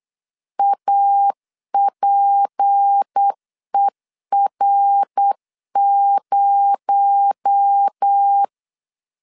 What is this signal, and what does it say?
Morse code, heard as a keyed tone, about 8 wpm, 800 Hz. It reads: APER0
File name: 8wpm